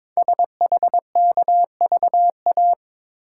Morse code, heard as a keyed tone, about 22 wpm, 690 Hz. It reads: SHKVA
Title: Code Groups - 5 Letters - 22wpm